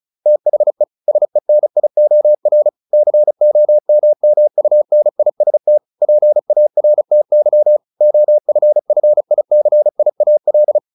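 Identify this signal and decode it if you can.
Morse code, heard as a keyed tone, about 34 wpm, 600 Hz. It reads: THE SENIOR COMMUNIST PARTY OFFICIAL
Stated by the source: Morse practice at 35 wpm from CW med Gnister, sending Ken Follett's novel Never in English.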